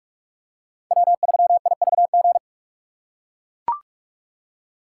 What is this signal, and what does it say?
Morse code, heard as a keyed tone, about 44 wpm, 700 Hz. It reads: W3IVG
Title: USA Call Signs - 8x Character Spacing - 45wpm